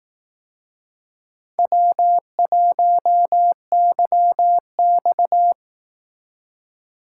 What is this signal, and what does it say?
Morse code, heard as a keyed tone, about 19 wpm, 700 Hz. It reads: W1YX
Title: Speed Racing - USA 1x2 Call Signs 15wpm